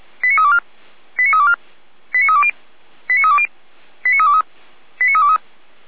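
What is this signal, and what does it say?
Five-tone selective call (ZVEI), heard as quick runs of five tones, tones of about 70 ms. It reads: 89324 89324 89320 89320 89323 89323
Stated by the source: ZVEI five-tone selective call 89315 received by SDR, logged by feuerwehr-orlamuende fire brigade.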